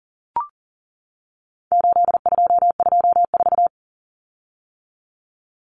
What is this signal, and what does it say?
Morse code, heard as a keyed tone, about 40 wpm, 700 Hz. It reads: 8224
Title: Common Cut Numbers - 4-Digit 40wpm